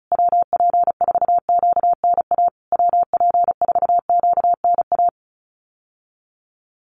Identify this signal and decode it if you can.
Morse code, heard as a keyed tone, about 34 wpm, 700 Hz. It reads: WP4QNA WP4QNA
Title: USA Call Signs Repeated 3x 35wpm